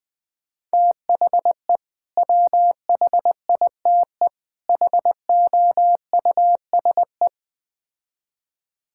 Morse code, heard as a keyed tone, about 21 wpm, 700 Hz. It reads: THE WHITE HOUSE
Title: Build A Phrase - 3-Word Phrases - 20wpm